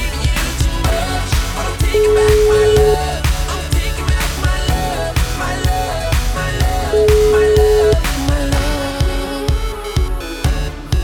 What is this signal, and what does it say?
Ringback tone (Europe, 425 Hz), with noise.